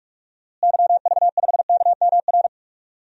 Morse code, heard as a keyed tone, about 46 wpm, 700 Hz. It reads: YV5KMR
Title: DXCC-100 Call Signs - 6 Characters - Repeated 3x - 45wpm